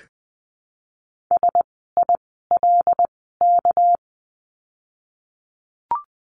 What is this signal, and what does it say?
Morse code, heard as a keyed tone, about 20 wpm, spaced as for 15 wpm, 700 Hz. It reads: SILK
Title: 199 Lesson 32f - Letter k - Words 20wpm